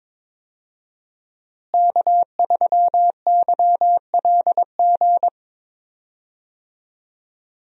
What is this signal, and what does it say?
Morse code, heard as a keyed tone, about 22 wpm, 700 Hz. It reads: K3YLG
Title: USA Call Signs - 5 Characters - Repeated 2x - 22wpm